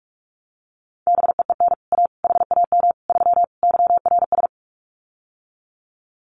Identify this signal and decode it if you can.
Morse code, heard as a keyed tone, about 44 wpm, 700 Hz. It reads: BEEN A HAM 3 YRS
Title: Common QSO Phrases Encore 45wpm